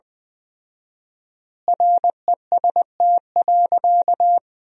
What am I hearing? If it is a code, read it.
Morse code, heard as a keyed tone, about 21 wpm, 700 Hz. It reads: REST.
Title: 4x Word-Spacing - Sentences from Top 500 Words - 20wpm